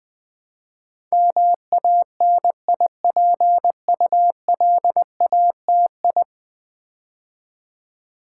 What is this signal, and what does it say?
Morse code, heard as a keyed tone, about 20 wpm, 700 Hz. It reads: MANIPULATI
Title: Build A Word - 5 Letters At A Time - 12-Letter Words - 20wpm